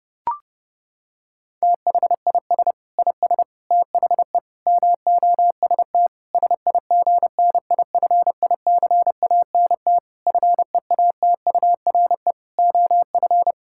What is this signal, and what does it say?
Morse code, heard as a keyed tone, about 32 wpm, 700 Hz. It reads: THIS IS THE MOST SIGNIFICANT FEATURE OF
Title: Sentences from Top 1000 Words - Review Part 6 - 30wpm